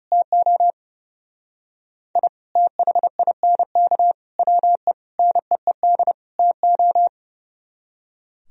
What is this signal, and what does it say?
Morse code, heard as a keyed tone, about 37 wpm, 700 Hz. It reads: TO I THINK WE NEED TO
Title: Speed-Racing - Sets of 5 Words 30wpm